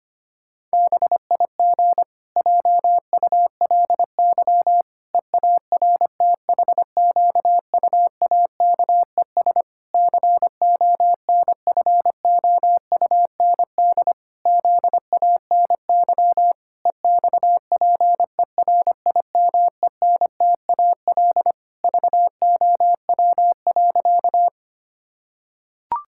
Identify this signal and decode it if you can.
Morse code, heard as a keyed tone, about 25 wpm, 700 Hz. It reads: BIG JULY EARTHQUAKES CONFOUND ZANY EXPERIMENTAL VOW.
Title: Pangram Challenge v2 - 25wpm